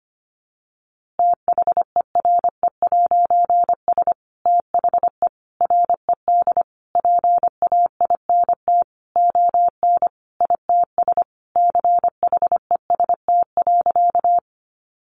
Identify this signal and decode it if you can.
Morse code, heard as a keyed tone, about 25 wpm, 700 Hz. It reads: THERE'S THE RED PAINT ON ITS CHEST.
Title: A Sound Of Thunder 25wpm